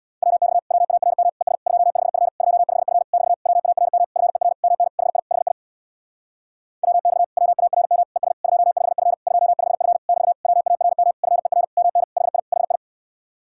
Morse code, heard as a keyed tone, about 66 wpm, 700 Hz. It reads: WX WARM ES 25F 25F <BT> WARM FER MN HI HI WX WARM ES 25F 25F <BT> WARM FER MN HI HI